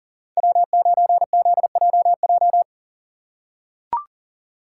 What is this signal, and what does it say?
Morse code, heard as a keyed tone, about 40 wpm, 700 Hz. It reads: W9ZJJ
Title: USA Call Signs - 5 Characters - 3x Character Spacing - 40wpm